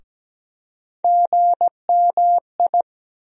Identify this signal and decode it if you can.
Morse code, heard as a keyed tone, about 17 wpm, 700 Hz. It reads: GMI
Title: USA Call Sign Postfixes - 3 Letters 17wpm